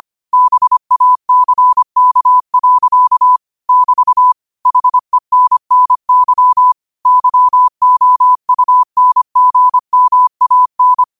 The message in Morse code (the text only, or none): BACK. <BT> HENNY YOUNGMAN